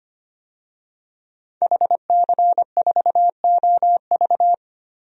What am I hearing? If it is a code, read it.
Morse code, heard as a keyed tone, about 25 wpm, 700 Hz. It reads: HC4OV
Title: DXCC-100 Call Signs - 5 Characters - 25wpm